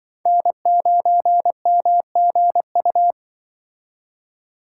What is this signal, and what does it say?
Morse code, heard as a keyed tone, about 24 wpm, 700 Hz. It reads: N9MGU
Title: Speed Racing - USA 1x3 Call Signs 20wpm